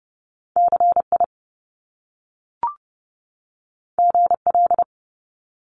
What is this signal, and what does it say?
Morse code, heard as a keyed tone, about 30 wpm, 700 Hz. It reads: CI GL
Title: Top Two Letter Combinations in Words 30wpm